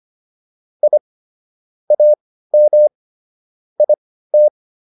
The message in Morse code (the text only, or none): I AM IT